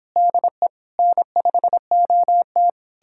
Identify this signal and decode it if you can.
Morse code, heard as a keyed tone, about 29 wpm, 700 Hz. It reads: DE N5OT